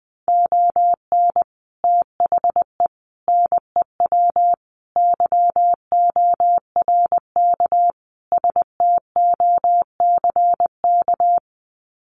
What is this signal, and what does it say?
Morse code, heard as a keyed tone, about 20 wpm, 700 Hz. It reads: ON THE NEW YORK STOCK
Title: Sets of 5 Words Encore 20wpm